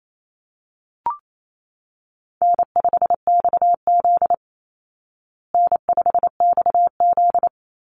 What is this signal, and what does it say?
Morse code, heard as a keyed tone, about 28 wpm, 700 Hz. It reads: N5XZ N5XZ
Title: Easy Call Signs Repeated 2x - 28wpm